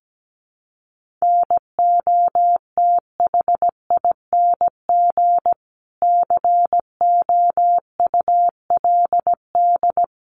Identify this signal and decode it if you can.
Morse code, heard as a keyed tone, about 17 wpm, 700 Hz. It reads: NOTHING COULD
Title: Sentences from Top 1400 Words 17wpm